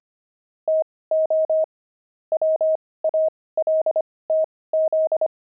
Morse code, heard as a keyed tone, about 25 wpm, spaced as for 18 wpm, 630 Hz. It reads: TO WALTZ
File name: pangram-fs-18-cs-25-devils